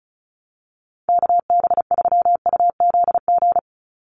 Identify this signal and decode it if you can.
Morse code, heard as a keyed tone, about 34 wpm, 700 Hz. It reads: KB3UZG
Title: USA Call Signs 35wpm